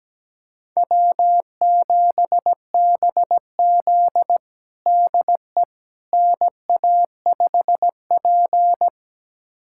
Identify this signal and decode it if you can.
Morse code, heard as a keyed tone, about 17 wpm, 700 Hz. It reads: W7BZ DE NA5P
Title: USA Callsign DE Callsign - 4 Characters - 17wpm